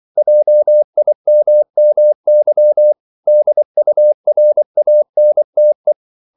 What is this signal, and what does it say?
Morse code, heard as a keyed tone, about 24 wpm, 600 Hz. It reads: JIMMY DURANTE
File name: CW / Practice / EnglishQuotations0224